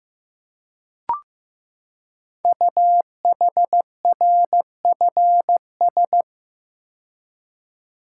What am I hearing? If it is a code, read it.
Morse code, heard as a keyed tone, about 15 wpm, 700 Hz. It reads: UHRFS
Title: Code Groups - 5 Letters - 15wpm